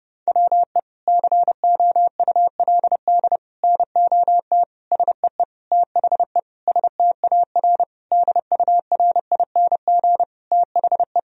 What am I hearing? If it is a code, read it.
Morse code, heard as a keyed tone, about 32 wpm, 700 Hz. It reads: WE COULD NOT SEE THE STAR DURING THE
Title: Sentences from Top 1000 Words - Review Part 2 - 30wpm